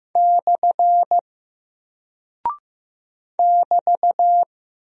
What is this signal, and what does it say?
Morse code, heard as a keyed tone, about 15 wpm, 700 Hz. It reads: / <BT>
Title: Single Character (Letter:Number:Punctuation:Prosign) Alt 15wpm